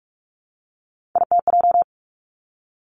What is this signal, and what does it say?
Morse code, heard as a keyed tone, about 46 wpm, 700 Hz. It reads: ITJ
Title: Three Challenging Letter-Number Combinations 45wpm